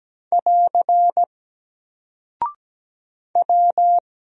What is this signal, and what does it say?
Morse code, heard as a keyed tone, about 17 wpm, 700 Hz. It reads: <AR> W
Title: Single Character (Letter:Number:Punctuation:Prosign) 17wpm